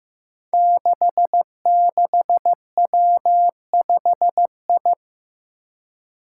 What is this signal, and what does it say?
Morse code, heard as a keyed tone, about 15 wpm, 700 Hz. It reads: 66W5I
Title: Five Challenging Letter-Number Combinations 15wpm